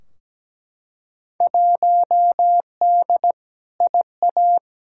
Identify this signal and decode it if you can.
Morse code, heard as a keyed tone, about 17 wpm, 700 Hz. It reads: 1D IA
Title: Field Day Exchanges - 17wpm